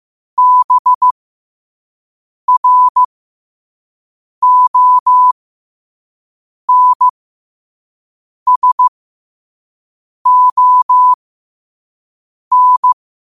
Morse code, heard as a keyed tone, about 15 wpm, spaced as for 5 wpm, 1000 Hz. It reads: BRONSON
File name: qotdfiles / 5wpm